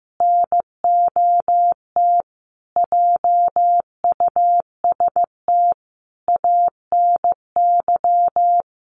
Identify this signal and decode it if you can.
Morse code, heard as a keyed tone, about 14 wpm, 700 Hz. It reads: NOT JUST ANY
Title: A Sound Of Thunder 15wpm